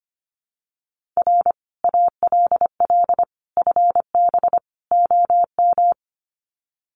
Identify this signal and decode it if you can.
Morse code, heard as a keyed tone, about 25 wpm, 700 Hz. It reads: R ALL FB OM
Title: Common QSO Phrases 25wpm